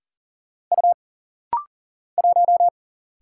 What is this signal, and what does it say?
Morse code, heard as a keyed tone, about 40 wpm, 700 Hz. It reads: U 1
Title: Single Letter-Number Alt - Rapid-Fire 40wpm